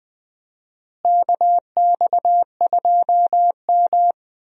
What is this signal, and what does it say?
Morse code, heard as a keyed tone, about 21 wpm, 700 Hz. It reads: KX2M
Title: USA Call Signs - 4 Characters - Repeated 2x - 20wpm